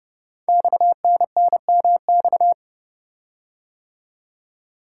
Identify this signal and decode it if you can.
Morse code, heard as a keyed tone, about 30 wpm, 700 Hz. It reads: XNNMX
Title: Code Groups - 5 Letters - 30wpm